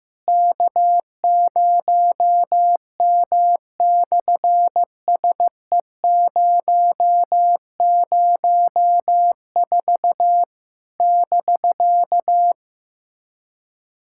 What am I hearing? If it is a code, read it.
Morse code, heard as a keyed tone, about 14 wpm, 700 Hz. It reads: K0M/SE004 <BK>
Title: SOTA QSO Practice 15wpm